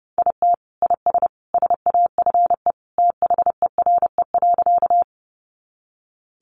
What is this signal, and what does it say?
Morse code, heard as a keyed tone, about 30 wpm, 700 Hz. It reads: IT IS SAFE THERE.
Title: Easy Phrases 30wpm